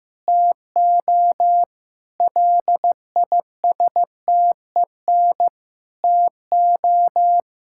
Morse code, heard as a keyed tone, about 16 wpm, 700 Hz. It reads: TO LISTEN TO
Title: Sentences from Top 1600 Words 15wpm